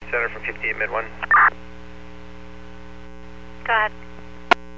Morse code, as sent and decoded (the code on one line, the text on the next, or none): none